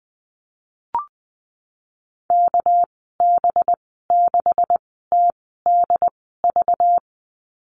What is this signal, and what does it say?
Morse code, heard as a keyed tone, about 20 wpm, spaced as for 15 wpm, 700 Hz. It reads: KB6TDV
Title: 200 Lesson 32g - Letter k - Call Signs 20wpm